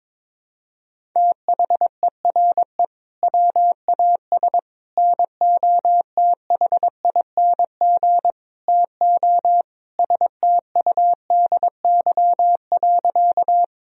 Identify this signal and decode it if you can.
Morse code, heard as a keyed tone, about 22 wpm, 700 Hz. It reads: THERE WAS NOTHING TO STUDY.
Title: Sentences from Top 1000 Words - Review Part 1 - 22wpm